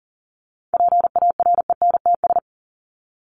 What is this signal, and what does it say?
Morse code, heard as a keyed tone, about 40 wpm, 700 Hz. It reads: PARENTS
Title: ES:S Suffix - Top 100 Words with Base Words 40wpm